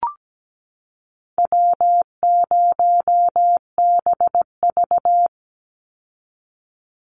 Morse code, heard as a keyed tone, about 17 wpm, 700 Hz. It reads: W0BV
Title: Easy Call Signs Repeated 2x - 17wpm